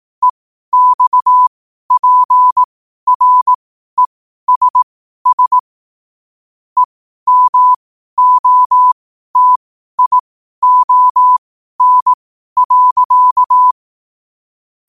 Morse code, heard as a keyed tone, about 18 wpm, spaced as for 12 wpm, 1000 Hz. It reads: EXPRESS EMOTION.
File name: qotdfiles / 12wpm